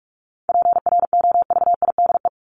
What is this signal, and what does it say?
Morse code, heard as a keyed tone, about 44 wpm, 700 Hz. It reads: PROVIDE
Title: ING Suffix - Top 100 Words with Base Words 45wpm